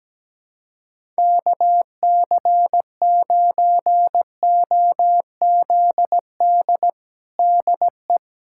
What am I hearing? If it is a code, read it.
Morse code, heard as a keyed tone, about 17 wpm, 700 Hz. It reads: KC9OZD DE
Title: USA Callsign DE Callsign 17wpm